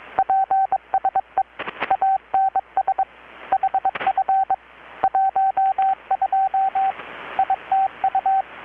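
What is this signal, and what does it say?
Morse code, heard as a keyed tone, about 22 wpm, 770 Hz, with radio noise. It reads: PSE ANS HF 12 ITU
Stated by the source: Morse from KPH's Night of Nights XX maritime radiotelegraphy event, received in Chiapas, Mexico about 3,800 km away on a Kenwood R-600 with a grounded 10 m long-wire antenna.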